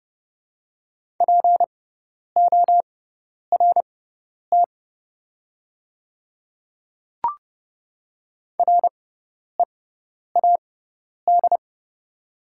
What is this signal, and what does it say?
Morse code, heard as a keyed tone, about 31 wpm, spaced as for 10 wpm, 700 Hz. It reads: PORT READ
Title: 6x Character Spacing - Top 500 Words - 30wpm